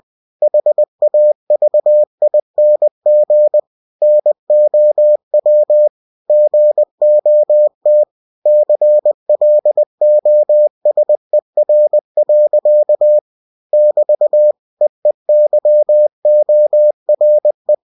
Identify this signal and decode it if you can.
Morse code, heard as a keyed tone, about 20 wpm, 600 Hz. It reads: HAVING NOW GOT CLOSER. <BT> EEYORE
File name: cw-books / winnie-the-pooh / winnie-20wpm-page-0054